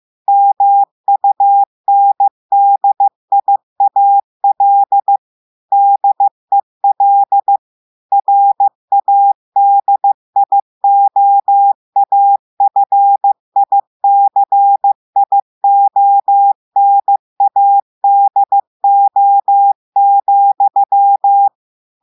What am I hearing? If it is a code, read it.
Morse code, heard as a keyed tone, about 15 wpm, 800 Hz. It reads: MUNDIAL DEL RADIOAFICIONADO,
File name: eh1ard-amateur-radio-day-2016 mf feed 1